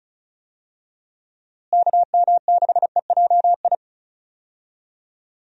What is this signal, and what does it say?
Morse code, heard as a keyed tone, about 34 wpm, 700 Hz. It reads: KM6EJI
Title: USA Call Signs - 6 Characters - Repeated 3x - 35wpm